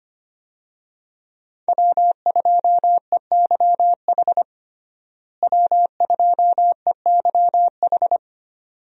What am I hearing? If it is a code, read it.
Morse code, heard as a keyed tone, about 25 wpm, 700 Hz. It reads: W2EYH W2EYH
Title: USA Call Signs - 5 Characters - Repeated 2x - 25wpm